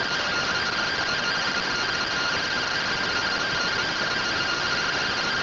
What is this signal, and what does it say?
Two-tone teleprinter signal, about 170 Hz shift.